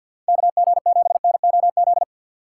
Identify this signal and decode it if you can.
Morse code, heard as a keyed tone, about 53 wpm, 700 Hz. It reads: KG7NOZ
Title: USA Call Signs - 6 Characters - 4x Character Spacing - 50wpm